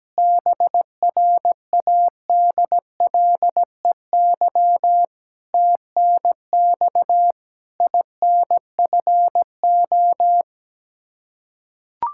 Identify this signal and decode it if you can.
Morse code, heard as a keyed tone, about 17 wpm, 700 Hz. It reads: BRADLEY TNX INFO
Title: Practice QSO - Exchange 3 - More Info - 17wpm